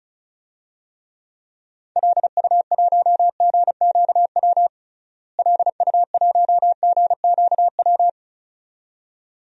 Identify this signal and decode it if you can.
Morse code, heard as a keyed tone, about 34 wpm, 700 Hz. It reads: LU1GQW LU1GQW
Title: DXCC-100 Call Signs - 6 Characters - Repeated 2x - 35wpm